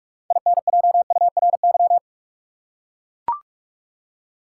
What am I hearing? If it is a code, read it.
Morse code, heard as a keyed tone, about 43 wpm, 700 Hz. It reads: INJURY
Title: Six Letter Words - 45wpm